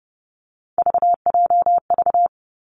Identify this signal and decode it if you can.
Morse code, heard as a keyed tone, about 30 wpm, 700 Hz. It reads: VJV
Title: Three Challenging Letter-Number Combinations 30wpm